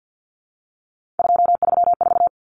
Morse code, heard as a keyed tone, about 50 wpm, 700 Hz. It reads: 234
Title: Numbers - 3-Digit 50wpm